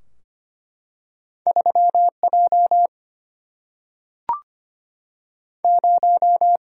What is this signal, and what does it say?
Morse code, heard as a keyed tone, about 25 wpm, 700 Hz. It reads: 3 J 0
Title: Single-Letter Single-Number 25wpm